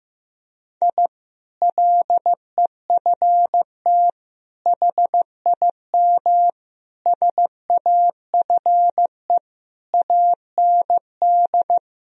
Morse code, heard as a keyed tone, about 15 wpm, 700 Hz. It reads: I LEFT HIM SAFE AND
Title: Sentences from Top 1000 Words - Review Part 1 - 15wpm